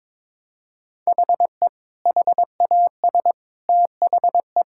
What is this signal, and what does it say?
Morse code, heard as a keyed tone, about 22 wpm, 700 Hz. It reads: HE HAS THE
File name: practice / 22wpm / Sentences with 3-Word Phrases - 22wpm